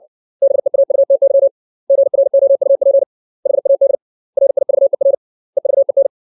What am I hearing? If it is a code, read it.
Morse code, heard as a keyed tone, about 60 wpm, 550 Hz. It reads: BEAUTY GROUP HAD RIVER EVEN
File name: 632 words / 632 Words 60WPM